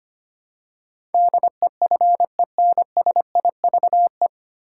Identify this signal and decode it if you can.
Morse code, heard as a keyed tone, about 25 wpm, 700 Hz. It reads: DEFENSIVE
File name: Nine Letter Words - 25wpm